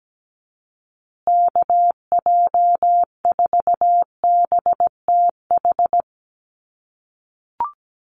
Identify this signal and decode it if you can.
Morse code, heard as a keyed tone, about 17 wpm, 700 Hz. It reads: KJ4BTH
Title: USA Call Signs Repeated 3x 17wpm